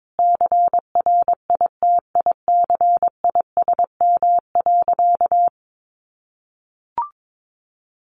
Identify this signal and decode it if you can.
Morse code, heard as a keyed tone, about 22 wpm, 700 Hz. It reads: CRITICISM.
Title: Sentences from Top 1300 Words 22wpm